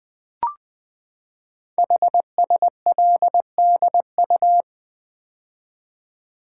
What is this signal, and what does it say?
Morse code, heard as a keyed tone, about 20 wpm, 700 Hz. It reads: HSLDU
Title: Code Groups - 5 Letter:Numbers - 20wpm